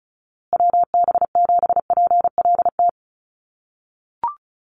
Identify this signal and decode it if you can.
Morse code, heard as a keyed tone, about 34 wpm, 700 Hz. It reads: WB7PLT
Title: USA Call Signs Repeated 3x 35wpm